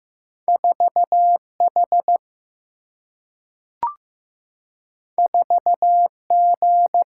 Morse code, heard as a keyed tone, about 15 wpm, 700 Hz. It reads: 4H 4G
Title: Single-Letter Single-Number 15wpm